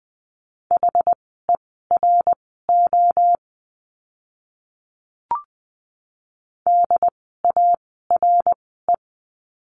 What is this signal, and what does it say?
Morse code, heard as a keyed tone, about 20 wpm, spaced as for 14 wpm, 700 Hz. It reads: HERO DARE
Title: 068 Lesson 12f - Letter d - Words 20wpm